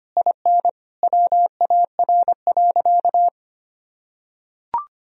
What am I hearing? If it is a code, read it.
Morse code, heard as a keyed tone, about 25 wpm, 700 Hz. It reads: IN WAR.
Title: Sentences from Top 1000 Words - Review Part 2 - 25wpm